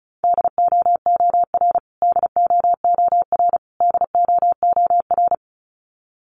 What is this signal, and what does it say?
Morse code, heard as a keyed tone, about 34 wpm, 700 Hz. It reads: DOOR DOOR DOOR
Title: Top 500 Words Repeated 35wpm